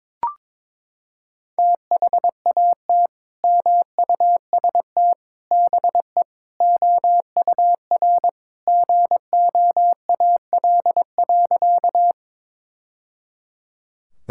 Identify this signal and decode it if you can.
Morse code, heard as a keyed tone, about 22 wpm, 700 Hz. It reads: THAT MUST BE OUR GOAL.
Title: Sentences from Top 1000 Words - Review Part 5 - 22wpm